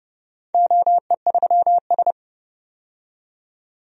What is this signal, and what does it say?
Morse code, heard as a keyed tone, about 30 wpm, 700 Hz. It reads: OE3S
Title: DXCC-100 Call Signs - 4 Characters - 30wpm